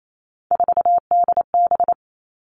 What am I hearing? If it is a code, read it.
Morse code, heard as a keyed tone, about 28 wpm, 700 Hz. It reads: 4DB
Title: All Cut Numbers - 3-Digit 28wpm